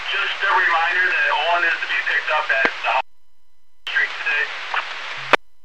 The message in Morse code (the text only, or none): none